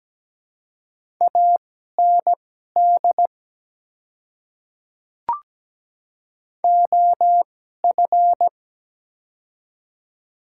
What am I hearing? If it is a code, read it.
Morse code, heard as a keyed tone, about 17 wpm, spaced as for 12 wpm, 700 Hz. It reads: AND OF
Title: 2x Character Spacing - Top 100 Words - 17wpm